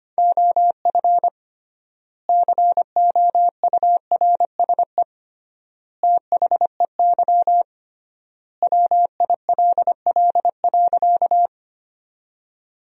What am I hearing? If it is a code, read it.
Morse code, heard as a keyed tone, about 25 wpm, 700 Hz. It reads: OF COURSE THEY WILL.
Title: 3x Word-Spacing - Sentences from Top 500 Words - 25wpm